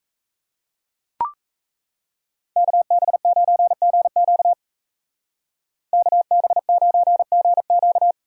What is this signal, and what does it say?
Morse code, heard as a keyed tone, about 34 wpm, 700 Hz. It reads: KB9GQ KB9GQ
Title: Speed Racing - USA 2x2 Call Signs 28wpm